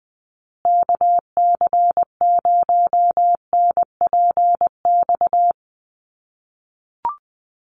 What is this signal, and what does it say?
Morse code, heard as a keyed tone, about 19 wpm, 700 Hz. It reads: KC0NPX
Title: USA Call Signs Repeated 2x 20wpm